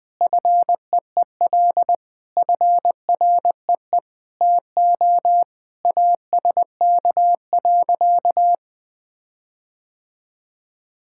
Morse code, heard as a keyed tone, about 20 wpm, 700 Hz. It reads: FEEL FREE TO ASK.
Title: Sentences with 8-Word Phrases - 20wpm